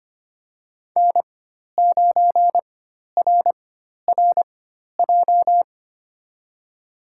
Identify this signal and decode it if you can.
Morse code, heard as a keyed tone, about 25 wpm, spaced as for 12 wpm, 700 Hz. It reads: N9RRJ
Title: USA Call Signs - 5 Characters - 4x Character Spacing - 25wpm